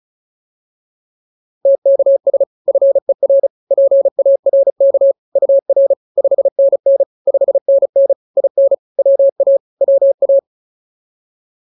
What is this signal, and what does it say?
Morse code, heard as a keyed tone, about 34 wpm, 560 Hz. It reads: TKS FER PARK UR 5NN 5NN IN WA WA